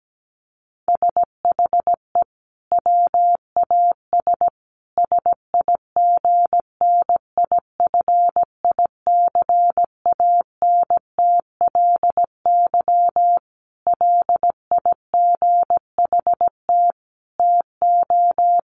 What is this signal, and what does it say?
Morse code, heard as a keyed tone, about 17 wpm, 700 Hz. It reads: SHE WAS SIGNIFICANTLY LIGHT TO
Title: Sentences from Top 1500 Words 17wpm